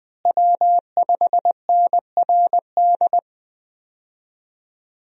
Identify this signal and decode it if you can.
Morse code, heard as a keyed tone, about 20 wpm, 700 Hz. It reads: W5NRD
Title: USA Call Signs - 5 Characters - Repeated 2x - 20wpm